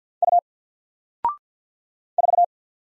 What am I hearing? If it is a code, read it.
Morse code, heard as a keyed tone, about 48 wpm, 700 Hz. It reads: U 4
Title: Single Character (Letter:Number:Punctuation:Prosign) Alt 50wpm